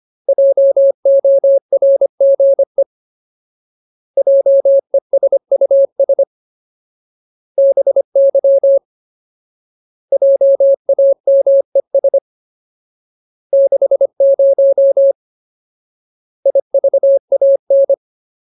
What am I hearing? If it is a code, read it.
Morse code, heard as a keyed tone, about 25 wpm, 550 Hz. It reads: JORGE JESUS BY JAMES 60 IVAN
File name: Ultimate Word List / 25WPM Ultimate